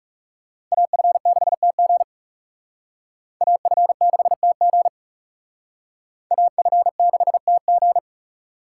Morse code, heard as a keyed tone, about 38 wpm, 700 Hz. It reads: AF6TG AF6TG AF6TG
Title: Speed Racing - USA Call Signs 30wpm